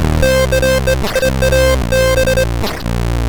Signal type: none